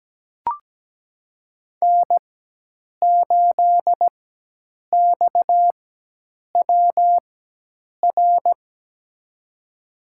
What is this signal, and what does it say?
Morse code, heard as a keyed tone, about 17 wpm, spaced as for 8 wpm, 700 Hz. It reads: N8XWR
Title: USA Call Signs - 4x Character Spacing - 17wpm